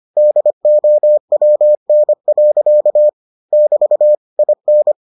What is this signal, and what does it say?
Morse code, heard as a keyed tone, about 25 wpm, 600 Hz. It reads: DOWN. <BT> IN